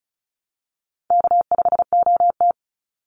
Morse code, heard as a keyed tone, about 34 wpm, 700 Hz. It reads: K5OT
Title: Easy Call Signs Repeated 2x - 35wpm